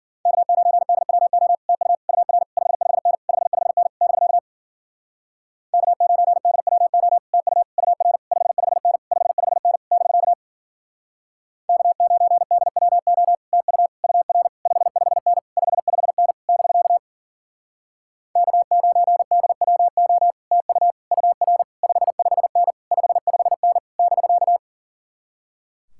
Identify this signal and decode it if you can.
Morse code, heard as a keyed tone, about 53 wpm, 700 Hz. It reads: K9DWO TU UR 55N 55N <BK> K9DWO TU UR 55N 55N <BK> K9DWO TU UR 55N 55N <BK> K9DWO TU UR 55N 55N <BK>